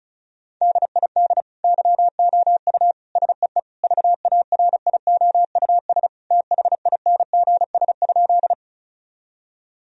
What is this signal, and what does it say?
Morse code, heard as a keyed tone, about 36 wpm, 700 Hz. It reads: DID YOU SEE VARIOUS THINGS?
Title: Sentences from Top 1000 Words - Review Part 4 - 35wpm